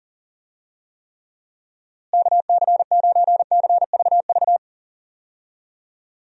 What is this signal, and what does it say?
Morse code, heard as a keyed tone, about 40 wpm, 700 Hz. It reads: KC9CVV
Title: USA Call Signs - 6 Characters - Repeated 3x - 40wpm